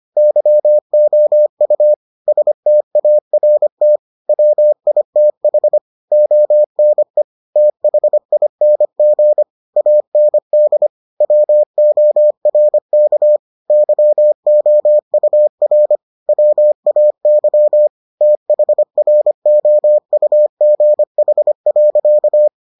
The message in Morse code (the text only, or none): YOU START WITH ONE THING AND WORK YOUR WAY THROUGH.